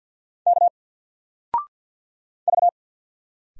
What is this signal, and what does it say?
Morse code, heard as a keyed tone, about 50 wpm, 700 Hz. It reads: K V
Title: Single Character (Letter:Number:Punctuation:Prosign) 50wpm